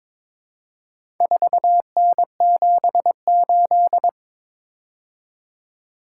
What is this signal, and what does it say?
Morse code, heard as a keyed tone, about 21 wpm, 700 Hz. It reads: 4N78
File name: practice / 22wpm / Four Letter-Number Combinations - 22wpm